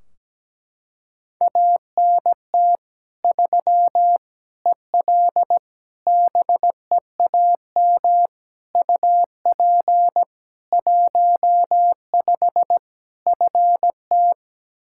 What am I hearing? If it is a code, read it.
Morse code, heard as a keyed tone, about 17 wpm, 700 Hz. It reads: ANT 3 EL BEAM UP 15 FT
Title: K1USN SST QSO Practice 17wpm